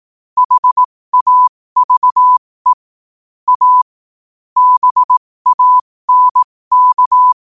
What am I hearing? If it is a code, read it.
Morse code, heard as a keyed tone, about 18 wpm, spaced as for 15 wpm, 1000 Hz. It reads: HAVE A BANK
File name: qotdfiles / 15wpm